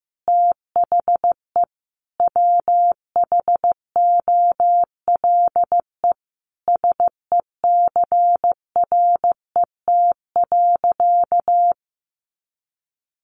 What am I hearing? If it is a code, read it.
Morse code, heard as a keyed tone, about 15 wpm, 700 Hz. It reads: THE WHOLE SECRET.
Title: Sentences from Top 1200 Words 15wpm